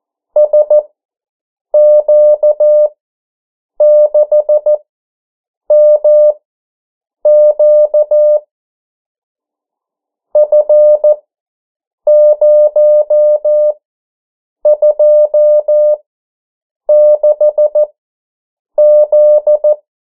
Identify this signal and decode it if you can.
Morse code, heard as a keyed tone, about 14 wpm, spaced as for 7 wpm, 600 Hz. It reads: SQ6MQ F026Z
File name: morse mp3 / 09 urok 8